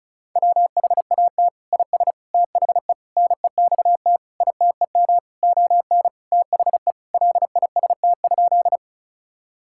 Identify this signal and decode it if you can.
Morse code, heard as a keyed tone, about 36 wpm, 700 Hz. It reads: WHAT IS THE NEXT ITEM ON THE LIST?
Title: Sentences from Top 1000 Words - Review Part 6 - 35wpm